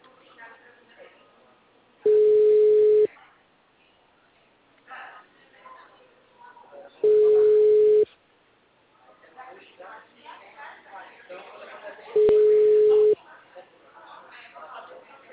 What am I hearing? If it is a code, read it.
Ringback tone (Europe, 425 Hz), with speech.
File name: out- -2003- - - .72104